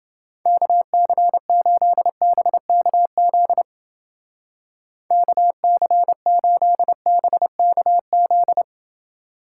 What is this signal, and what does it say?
Morse code, heard as a keyed tone, about 28 wpm, 700 Hz. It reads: KC8BKZ KC8BKZ